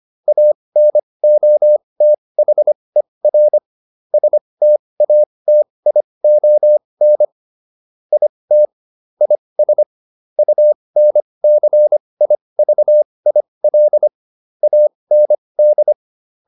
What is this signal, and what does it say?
Morse code, heard as a keyed tone, about 25 wpm, spaced as for 20 wpm, 600 Hz. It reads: ANOTHER STATION IT IS UNCIVIL AND